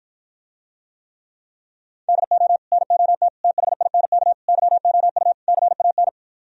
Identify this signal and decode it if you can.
Morse code, heard as a keyed tone, about 52 wpm, 700 Hz. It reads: DO NOT THINK YOU CAN